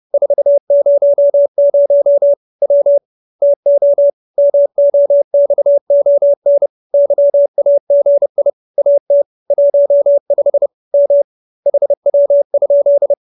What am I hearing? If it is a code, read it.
Morse code, heard as a keyed tone, about 30 wpm, 570 Hz. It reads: 400 W TO MOXON YAGI AT 15 M HW?
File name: QSO310 30